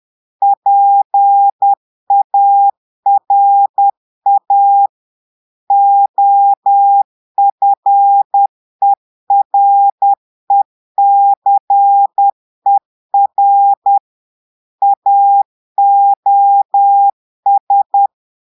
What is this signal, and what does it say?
Morse code, heard as a keyed tone, about 10 wpm, 800 Hz. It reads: PARA OFERECER AOS
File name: portugais texte en clair 2 vitesse 10 mots minutes